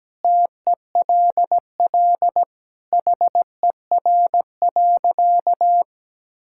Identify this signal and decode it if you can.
Morse code, heard as a keyed tone, about 17 wpm, 700 Hz. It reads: TELL HER.
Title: Sentences from Top 1000 Words - Review Part 5 - 17wpm